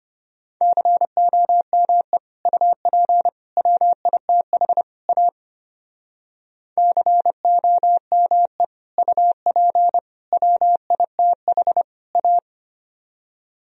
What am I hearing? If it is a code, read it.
Morse code, heard as a keyed tone, about 33 wpm, 700 Hz. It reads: COME UP WITH A COME UP WITH A